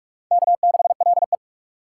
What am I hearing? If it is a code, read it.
Morse code, heard as a keyed tone, about 44 wpm, 700 Hz. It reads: K6LE